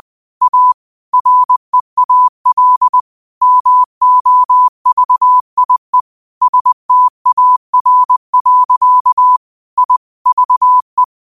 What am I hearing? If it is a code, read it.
Morse code, heard as a keyed tone, about 20 wpm, 1000 Hz. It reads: A REAL MOVIE STAR. I VE